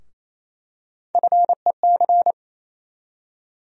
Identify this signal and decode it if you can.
Morse code, heard as a keyed tone, about 28 wpm, 700 Hz. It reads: FEC
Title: USA Call Sign Postfixes - 3 Letters 28wpm